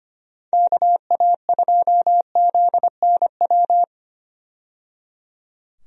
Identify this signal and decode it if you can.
Morse code, heard as a keyed tone, about 25 wpm, 700 Hz. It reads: KA2ZNW
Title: USA Call Signs - 6 Characters - 25wpm